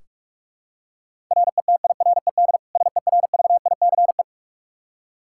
Morse code, heard as a keyed tone, about 46 wpm, 700 Hz. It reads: RETIRED SERVICE
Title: Practice QSO - Exchange 3 - More Info - 45wpm